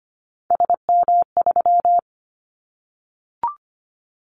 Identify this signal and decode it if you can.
Morse code, heard as a keyed tone, about 25 wpm, 700 Hz. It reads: SM3
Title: Callsign Prefixes for Top 25 DXCC - 25wpm